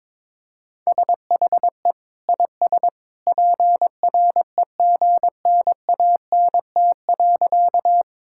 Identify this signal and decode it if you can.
Morse code, heard as a keyed tone, about 22 wpm, 700 Hz. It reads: SHE IS PREGNANT.
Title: Sentences from Top 1000 Words - Review Part 6 - 22wpm